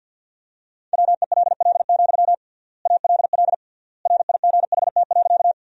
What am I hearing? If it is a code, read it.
Morse code, heard as a keyed tone, about 51 wpm, 700 Hz. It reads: WELL, ALL RIGHT.
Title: 3x Word-Spacing - Sentences from Top 200 Words - 50wpm